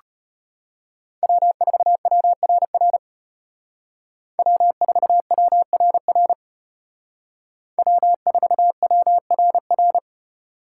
Morse code, heard as a keyed tone, about 32 wpm, 700 Hz. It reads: W4WRR W4WRR W4WRR